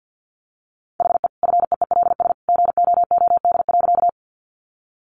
Morse code, heard as a keyed tone, about 50 wpm, 700 Hz. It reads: HE FEELS GOOD.
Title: Additional Easy Phrases 50wpm